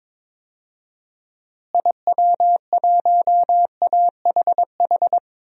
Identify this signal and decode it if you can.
Morse code, heard as a keyed tone, about 22 wpm, 700 Hz. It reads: IW1AHH
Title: DXCC-100 Call Signs - 6 Characters - Repeated 2x - 22wpm